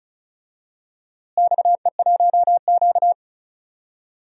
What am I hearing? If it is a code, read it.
Morse code, heard as a keyed tone, about 34 wpm, 700 Hz. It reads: XE1Q
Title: DXCC-100 Call Signs - 4 Characters - Repeated 2x - 35wpm